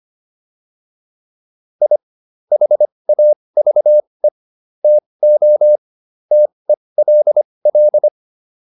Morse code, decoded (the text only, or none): I HAVE TO TELL